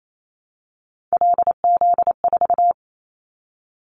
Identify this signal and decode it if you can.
Morse code, heard as a keyed tone, about 28 wpm, 700 Hz. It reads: LZ4
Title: Callsign Prefixes for Top 25 DXCC - 28wpm